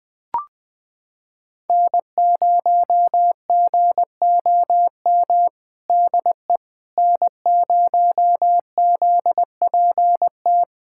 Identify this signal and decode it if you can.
Morse code, heard as a keyed tone, about 21 wpm, 700 Hz. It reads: N0GOM DE N0ZPT
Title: USA Callsign DE Callsign - 5 Characters - 20wpm